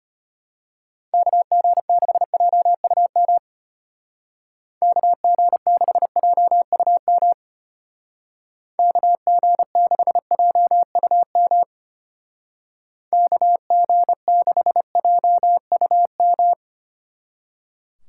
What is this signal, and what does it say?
Morse code, heard as a keyed tone, about 30 wpm, 700 Hz. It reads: KG6JUM KG6JUM KG6JUM KG6JUM
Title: Speed Racing - USA 2x3 Call Signs 25wpm